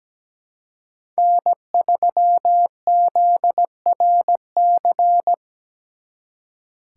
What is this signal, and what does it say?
Morse code, heard as a keyed tone, about 20 wpm, 700 Hz. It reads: N3ZRC
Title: Speed Racing - USA 1x3 Call Signs 17wpm